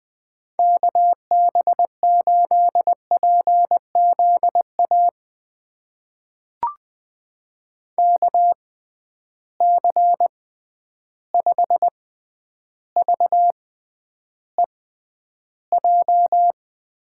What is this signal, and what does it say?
Morse code, heard as a keyed tone, about 20 wpm, spaced as for 10 wpm, 700 Hz. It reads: KB8PZA KC5VEJ
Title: USA Call Signs - 6 Characters - 6x Character Spacing - 20wpm